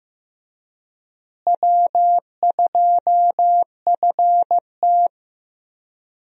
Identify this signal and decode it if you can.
Morse code, heard as a keyed tone, about 15 wpm, 700 Hz. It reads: W2FT
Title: USA Call Signs - 4 Characters - Repeated 3x - 15wpm